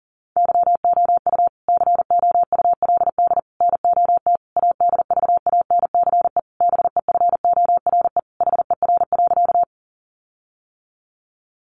Morse code, heard as a keyed tone, about 40 wpm, 700 Hz. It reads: YOU COULD NOT ADVANCE BEFORE HER.
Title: Sentences from Top 1400 Words 40wpm